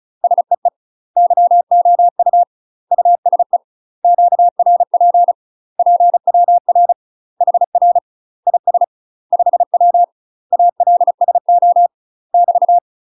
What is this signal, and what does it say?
Morse code, heard as a keyed tone, about 34 wpm, 700 Hz. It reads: SEE YOU USE QRP PWR HR IS 5W ALSO <BT>
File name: UK MASTER QSOs TRAINING / 35WPM / UK MASTER QSOs 0040